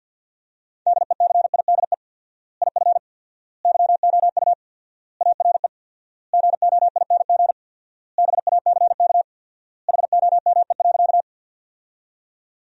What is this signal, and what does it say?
Morse code, heard as a keyed tone, about 52 wpm, 700 Hz. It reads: DECIDE IF YOU ARE GOING BACK HOME.
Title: 4x Word-Spacing - Sentences from Top 500 Words - 50wpm